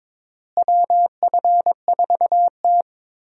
Morse code, heard as a keyed tone, about 22 wpm, 700 Hz. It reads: WF4T
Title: USA Call Signs - 4 Characters - Repeated 3x - 22wpm